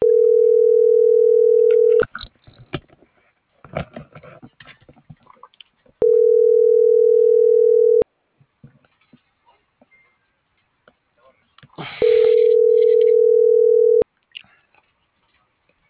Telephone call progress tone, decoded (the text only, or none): ringback tone